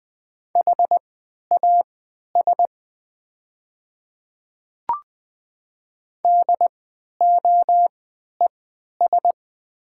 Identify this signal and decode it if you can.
Morse code, heard as a keyed tone, about 20 wpm, spaced as for 11 wpm, 700 Hz. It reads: HAS DOES
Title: 3x Character Spacing - Top 200 Words - 20wpm